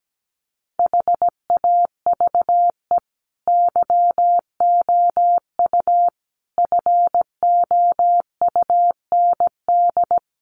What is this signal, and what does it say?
Morse code, heard as a keyed tone, about 17 wpm, 700 Hz. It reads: HAVE YOU FOUND
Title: Sentences from Top 300 Words 17wpm